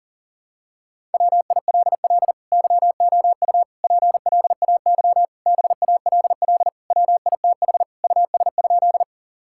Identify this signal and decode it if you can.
Morse code, heard as a keyed tone, about 40 wpm, 700 Hz. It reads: WILL YOU PLAY BALL WITH US?
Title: Sentences from Top 1000 Words - Review Part 6 - 40wpm